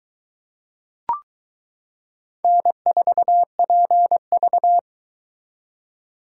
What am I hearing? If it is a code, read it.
Morse code, heard as a keyed tone, about 22 wpm, 700 Hz. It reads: N4PV